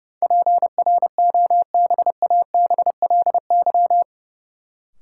Morse code, heard as a keyed tone, about 31 wpm, 700 Hz. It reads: PROBABLY
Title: Sentences from Top 700 Words - New Word Drill - 30wpm